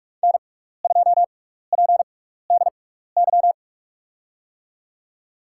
Morse code, heard as a keyed tone, about 44 wpm, spaced as for 15 wpm, 700 Hz. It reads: N2PDY
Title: USA Call Signs - 6x Character Spacing - 45wpm